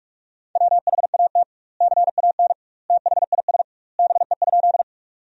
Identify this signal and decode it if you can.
Morse code, heard as a keyed tone, about 46 wpm, 700 Hz. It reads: WHAT CAN THIS BE?
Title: 2x Word-Spacing - Sentences from Top 100 Words - 45wpm